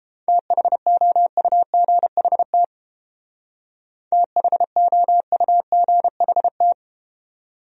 Morse code, heard as a keyed tone, about 33 wpm, 700 Hz. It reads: THOUGHT THOUGHT